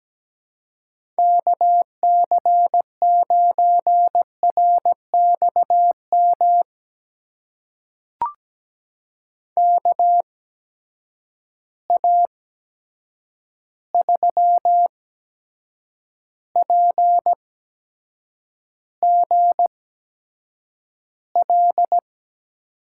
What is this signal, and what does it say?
Morse code, heard as a keyed tone, about 17 wpm, spaced as for 7 wpm, 700 Hz. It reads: KC9RXM KA3PGL
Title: USA Call Signs - 6 Characters - 8x Character Spacing - 17wpm